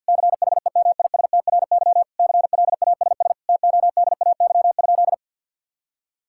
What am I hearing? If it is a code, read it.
Morse code, heard as a keyed tone, about 50 wpm, 700 Hz. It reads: CHEMISTRY CLASS TODAY?
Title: Sentences from Top 1000 Words - Review Part 6 - 50wpm